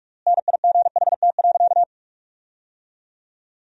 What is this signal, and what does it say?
Morse code, heard as a keyed tone, about 47 wpm, 700 Hz. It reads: NIGHT.
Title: 4x Word-Spacing - Sentences from Top 300 Words - 45wpm